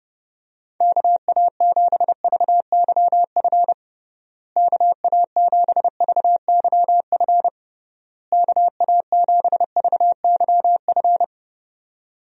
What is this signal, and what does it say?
Morse code, heard as a keyed tone, about 32 wpm, 700 Hz. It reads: KA7VYF KA7VYF KA7VYF
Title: USA Call Signs - 6 Characters - Repeated 3x - 30wpm